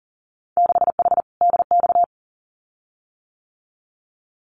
Morse code, heard as a keyed tone, about 40 wpm, 700 Hz. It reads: 6H DX
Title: Winter Field Day Exchanges Encore 40wpm